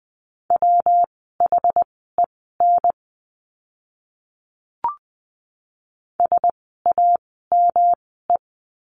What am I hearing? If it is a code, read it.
Morse code, heard as a keyed tone, about 20 wpm, spaced as for 15 wpm, 700 Hz. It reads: WHEN SAME
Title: 192 Lesson 31f - Letter v - Words 20wpm